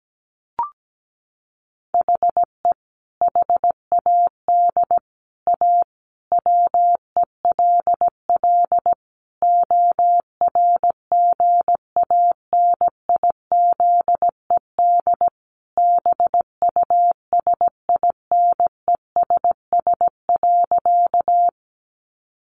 Morse code, heard as a keyed tone, about 17 wpm, 700 Hz. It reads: HE HAD A WELL ORGANIZED BUSINESS.
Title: Sentences from Top 1300 Words 17wpm